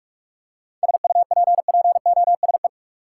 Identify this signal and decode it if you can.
Morse code, heard as a keyed tone, about 41 wpm, 700 Hz. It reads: SUPPOSE
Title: Seven Letter Words - 45wpm